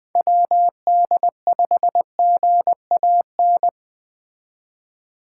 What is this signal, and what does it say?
Morse code, heard as a keyed tone, about 20 wpm, 700 Hz. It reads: WD5GAN
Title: Speed Racing - USA 2x3 Call Signs 17wpm